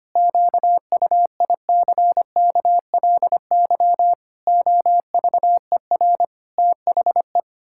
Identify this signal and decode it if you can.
Morse code, heard as a keyed tone, about 25 wpm, 700 Hz. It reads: QUICKLY OVER THE
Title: Pangram Challenge v2 - 25wpm